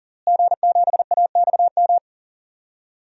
Morse code, heard as a keyed tone, about 40 wpm, 700 Hz. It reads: G7AXM
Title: DXCC-100 Call Signs - 5 Characters - Repeated 2x - 40wpm